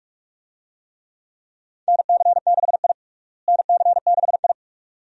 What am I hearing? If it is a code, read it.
Morse code, heard as a keyed tone, about 43 wpm, 700 Hz. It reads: NC6I NC6I
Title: USA Call Signs - 4 Characters - Repeated 2x - 45wpm